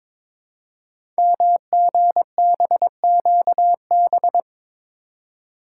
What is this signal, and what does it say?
Morse code, heard as a keyed tone, about 22 wpm, 700 Hz. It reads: MGBQB
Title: Code Groups - 5 Letters - 22wpm